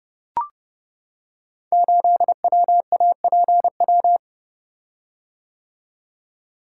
Morse code, heard as a keyed tone, about 28 wpm, 690 Hz. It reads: 8WAPW
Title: Code Groups - 5 Letter:Numbers - 30wpm